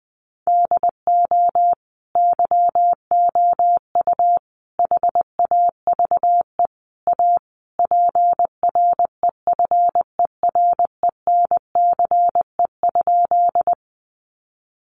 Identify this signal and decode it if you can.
Morse code, heard as a keyed tone, about 20 wpm, 700 Hz. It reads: DO YOU HAVE A PREFERENCE?
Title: Sentences from Top 1500 Words 20wpm